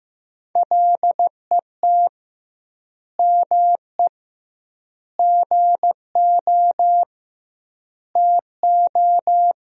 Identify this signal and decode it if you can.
Morse code, heard as a keyed tone, about 15 wpm, 700 Hz. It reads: LET ME GO TO
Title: 2x Word-Spacing - Sentences from Top 300 Words - 15wpm